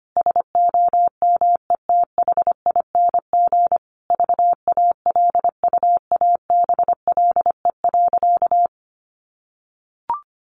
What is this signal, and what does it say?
Morse code, heard as a keyed tone, about 25 wpm, 700 Hz. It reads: SOMETHING VALUABLE.
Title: Sentences from Top 1400 Words 25wpm